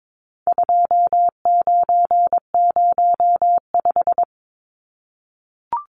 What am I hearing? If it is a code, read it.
Morse code, heard as a keyed tone, about 22 wpm, 700 Hz. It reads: 2905
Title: Numbers - 4-Digit 22wpm